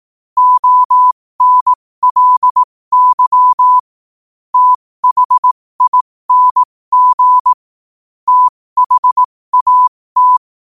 Morse code, heard as a keyed tone, about 18 wpm, spaced as for 15 wpm, 1000 Hz. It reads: ONLY THING THAT
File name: qotdfiles / 15wpm